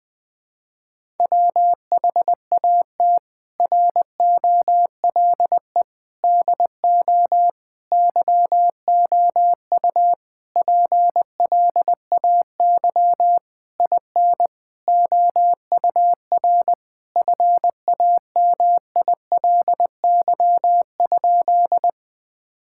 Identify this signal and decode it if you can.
Morse code, heard as a keyed tone, about 20 wpm, 700 Hz. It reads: WHAT ROLE DO YOU PLAY IN OUR FAMILY?
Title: Sentences from Top 1000 Words - Review Part 6 - 20wpm